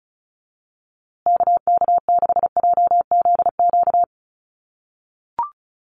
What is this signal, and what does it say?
Morse code, heard as a keyed tone, about 34 wpm, 700 Hz. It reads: KK6JZQ
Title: USA Call Signs Repeated 2x 35wpm